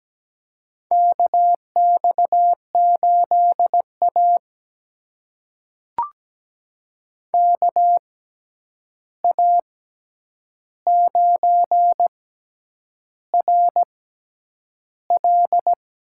Morse code, heard as a keyed tone, about 17 wpm, spaced as for 8 wpm, 700 Hz. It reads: KX8A KA9RL
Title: USA Call Signs - 6x Character Spacing - 17wpm